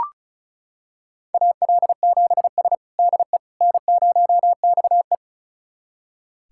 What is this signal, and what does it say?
Morse code, heard as a keyed tone, about 36 wpm, 700 Hz. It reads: AL7S DE N0XE
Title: USA Callsign DE Callsign - 4 Characters - 35wpm